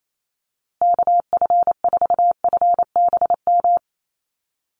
Morse code, heard as a keyed tone, about 28 wpm, 700 Hz. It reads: KF4FBM
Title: USA Call Signs Repeated 2x 28wpm